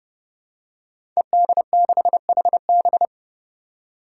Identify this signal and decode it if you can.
Morse code, heard as a keyed tone, about 30 wpm, 700 Hz. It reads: ED6HB
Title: Five Challenging Letter-Number Combinations 30wpm